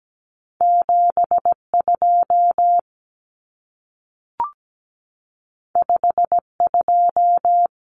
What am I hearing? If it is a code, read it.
Morse code, heard as a keyed tone, about 17 wpm, 700 Hz. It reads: 7 2 5 2
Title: Common Cut Numbers - 2-Digit 17wpm